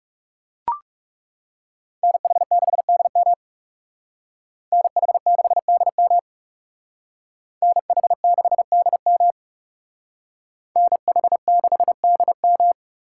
Morse code, heard as a keyed tone, about 38 wpm, 700 Hz. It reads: NH6DM NH6DM NH6DM NH6DM